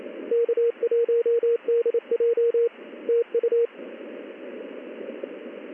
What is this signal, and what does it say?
Morse code, heard as a keyed tone, about 28 wpm, 470 Hz. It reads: K1DJ TU
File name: contests / ss-ssb-k5tr- -1700